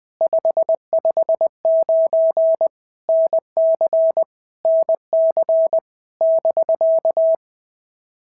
Morse code, heard as a keyed tone, about 20 wpm, 650 Hz. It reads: 559 NC NC <BK>
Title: POTA QSO Practice - 20wpm